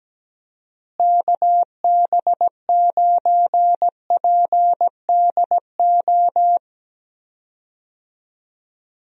Morse code, heard as a keyed tone, about 17 wpm, 700 Hz. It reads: KB9PDO
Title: USA Call Signs - 6 Characters - Repeated 3x - 17wpm